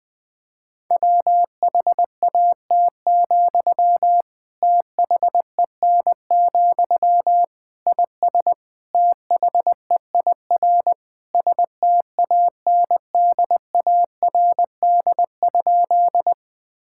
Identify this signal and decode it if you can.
Morse code, heard as a keyed tone, about 19 wpm, 700 Hz. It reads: WHAT, THEN, IS THEIR STANDARD?
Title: Sentences from Top 1000 Words - Review Part 6 - 20wpm